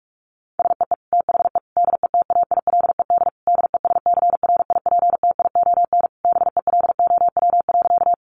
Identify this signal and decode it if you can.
Morse code, heard as a keyed tone, about 44 wpm, 700 Hz. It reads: SEE THE DETAILED DESCRIPTION BELOW.
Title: Sentences from Top 1500 Words 45wpm